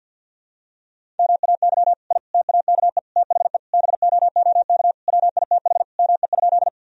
Morse code, heard as a keyed tone, about 52 wpm, 700 Hz. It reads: MAY I TAKE THE BOOK WITH ME?
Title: Sentences from Top 1000 Words - Review Part 1 - 50wpm